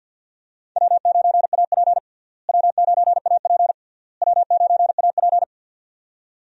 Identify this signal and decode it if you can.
Morse code, heard as a keyed tone, about 52 wpm, 700 Hz. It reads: W9AP W9AP W9AP
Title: USA Call Signs - 4 Characters - Repeated 3x - 50wpm